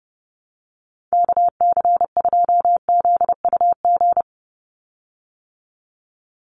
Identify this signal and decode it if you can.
Morse code, heard as a keyed tone, about 30 wpm, 700 Hz. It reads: KC2ZUG
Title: USA Call Signs 30wpm